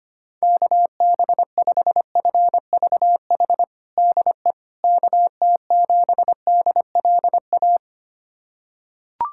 Morse code, heard as a keyed tone, about 25 wpm, 700 Hz. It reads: KB5FVH DE KT7DLA
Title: USA Callsign DE Callsign 25wpm